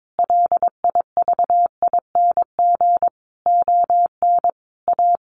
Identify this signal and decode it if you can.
Morse code, heard as a keyed tone, about 22 wpm, 700 Hz. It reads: LIVING ON A
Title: Sentences from Top 1200 Words 22wpm